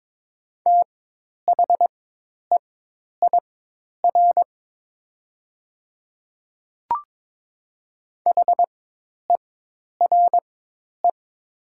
Morse code, heard as a keyed tone, about 22 wpm, spaced as for 10 wpm, 700 Hz. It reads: THEIR HERE
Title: 4x Character Spacing - Top 300 Words - 22wpm